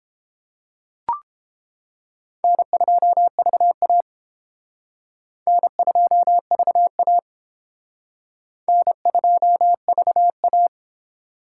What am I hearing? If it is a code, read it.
Morse code, heard as a keyed tone, about 28 wpm, 700 Hz. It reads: N2VA N2VA N2VA